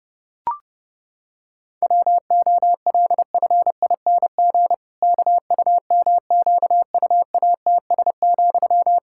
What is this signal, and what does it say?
Morse code, heard as a keyed tone, about 32 wpm, 700 Hz. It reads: WOLFING KUMQUATS,
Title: Pangram Challenge v2 - 30wpm